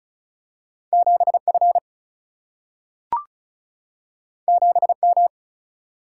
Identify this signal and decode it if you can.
Morse code, heard as a keyed tone, about 36 wpm, 700 Hz. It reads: 7 F 7 M
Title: Single-Letter Single-Number 35wpm